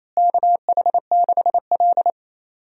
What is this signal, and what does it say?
Morse code, heard as a keyed tone, about 28 wpm, 700 Hz. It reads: KH6L
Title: DXCC-100 Call Signs - 4 Characters - Repeated 3x - 28wpm